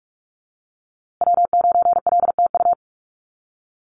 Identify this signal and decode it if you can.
Morse code, heard as a keyed tone, about 44 wpm, 700 Hz. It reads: W9LTU
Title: USA Call Signs Repeated 3x 45wpm